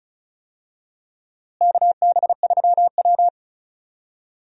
Morse code, heard as a keyed tone, about 36 wpm, 700 Hz. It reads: KB3W
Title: USA Call Signs - 4 Characters - Repeated 2x - 35wpm